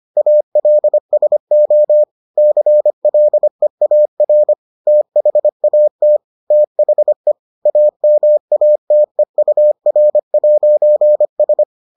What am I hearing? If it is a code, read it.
Morse code, heard as a keyed tone, about 25 wpm, 600 Hz. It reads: ALSO CLEAR THAT THE AMATEUR'S